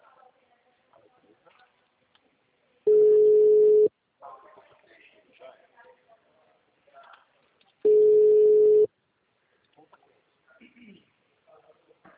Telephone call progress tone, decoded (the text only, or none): ringback tone